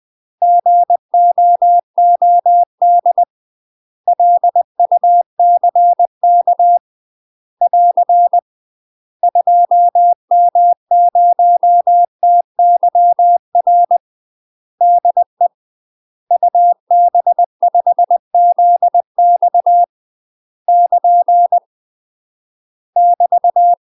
Morse code, heard as a keyed tone, about 20 wpm, 700 Hz. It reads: GOOD LUCK <AR> 2M0TYR DE UB5ZX <KN> <BT>
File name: UK MASTER QSOs TRAINING / 20WPM / UK MASTER QSOs 0004